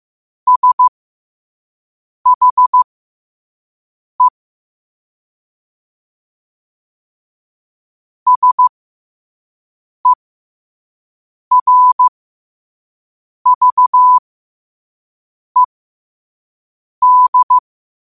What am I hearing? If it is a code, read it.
Morse code, heard as a keyed tone, about 15 wpm, spaced as for 5 wpm, 1000 Hz. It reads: SHE SERVED
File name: qotdfiles / 5wpm